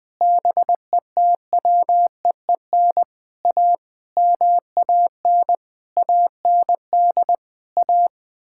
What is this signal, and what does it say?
Morse code, heard as a keyed tone, about 20 wpm, 700 Hz. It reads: BETWEEN A MAN AND A
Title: Sentences with 6-Word Phrases - 20wpm